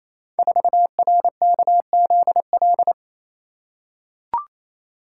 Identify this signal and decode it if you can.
Morse code, heard as a keyed tone, about 27 wpm, 700 Hz. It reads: 4RKZL
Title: Code Groups - 5 Letter:Numbers - 28wpm